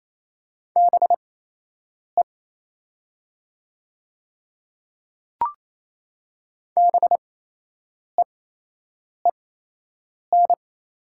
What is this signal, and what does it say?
Morse code, heard as a keyed tone, about 28 wpm, spaced as for 8 wpm, 700 Hz. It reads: BE BEEN
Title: 8x Character Spacing - Top 100 Words - 28wpm